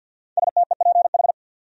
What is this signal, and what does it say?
Morse code, heard as a keyed tone, about 48 wpm, 700 Hz. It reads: STEPH
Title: USA Common Names - Max 5 Letters - 50wpm